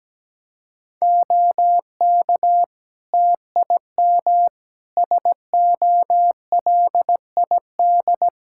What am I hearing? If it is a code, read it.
Morse code, heard as a keyed tone, about 17 wpm, 700 Hz. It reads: OK TIM SOLID
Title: Practice QSO - Exchange 3 - More Info - 17wpm